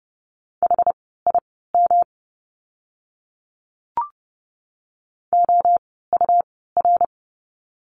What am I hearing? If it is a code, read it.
Morse code, heard as a keyed tone, about 30 wpm, spaced as for 17 wpm, 700 Hz. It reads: HIM OUR
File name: course / 30:10wpm / 115 Lesson 19f - Letter w - Words 30wpm